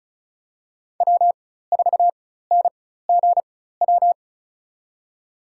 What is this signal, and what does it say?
Morse code, heard as a keyed tone, about 34 wpm, spaced as for 16 wpm, 700 Hz. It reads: W4NGW
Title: USA Call Signs - 5 Characters - 4x Character Spacing - 35wpm